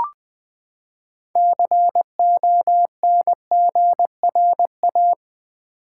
Morse code, heard as a keyed tone, about 20 wpm, 700 Hz. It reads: CONGRA
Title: Build A Word - 6 Letters At A Time - 12-Letter Words - 20wpm